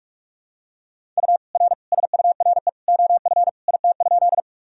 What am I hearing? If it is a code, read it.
Morse code, heard as a keyed tone, about 47 wpm, 700 Hz. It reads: U R SURE OF IT?
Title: Sentences from Top 300 Words and Abbreviations 45wpm